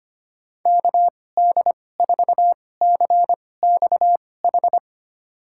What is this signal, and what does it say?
Morse code, heard as a keyed tone, about 25 wpm, spaced as for 18 wpm, 700 Hz. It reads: KD4CXH
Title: USA Call Signs - 6 Characters - 2x Character Spacing - 25wpm